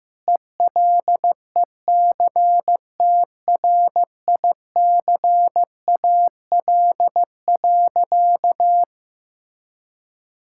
Morse code, heard as a keyed tone, about 16 wpm, 700 Hz. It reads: ELECTRICAL.
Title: Sentences from Top 1000 Words - Review Part 6 - 15wpm